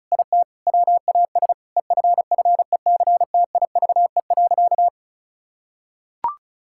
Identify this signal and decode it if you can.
Morse code, heard as a keyed tone, about 36 wpm, 700 Hz. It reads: IT WAS EFFECTIVE.
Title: Sentences from Top 1000 Words - Review Part 6 - 35wpm